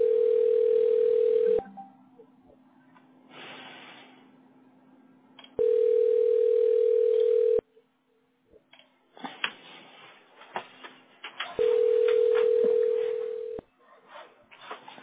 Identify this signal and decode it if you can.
Ringback tone (North America), 440 + 480 Hz.